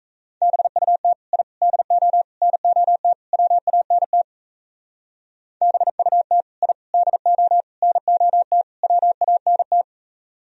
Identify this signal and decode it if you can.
Morse code, heard as a keyed tone, about 35 wpm, 700 Hz. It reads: BUT I DO NOT WANT BUT I DO NOT WANT